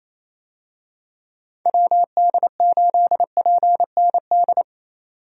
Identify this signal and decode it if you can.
Morse code, heard as a keyed tone, about 27 wpm, 700 Hz. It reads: WD8PND
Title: USA Call Signs - 6 Characters - 28wpm